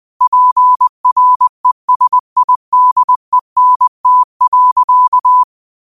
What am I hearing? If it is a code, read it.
Morse code, heard as a keyed tone, about 20 wpm, 1000 Hz. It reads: PRESIDENT.
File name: qotdfiles / 20wpm